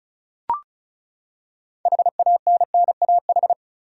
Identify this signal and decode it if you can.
Morse code, heard as a keyed tone, about 36 wpm, 700 Hz. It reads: HANNAH
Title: USA Common Names - 35wpm